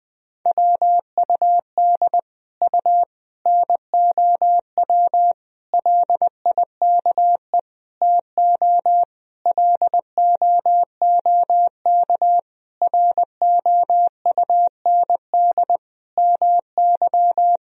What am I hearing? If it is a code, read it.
Morse code, heard as a keyed tone, about 20 wpm, 700 Hz. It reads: WUD U NOW LIKE TO LOOK ROUND MY
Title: Sentences from Top 200 Words and Abbreviations 20wpm